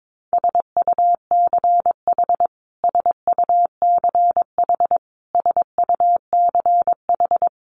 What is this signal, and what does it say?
Morse code, heard as a keyed tone, about 22 wpm, 700 Hz. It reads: SUCH SUCH SUCH
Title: Top 500 Words Repeated 22wpm